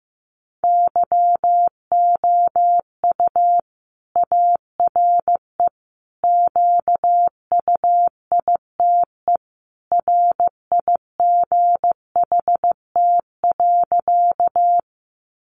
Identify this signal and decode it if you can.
Morse code, heard as a keyed tone, about 15 wpm, 700 Hz. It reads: YOU ARE QUITE RIGHT.
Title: Sentences from Top 700 Words 15wpm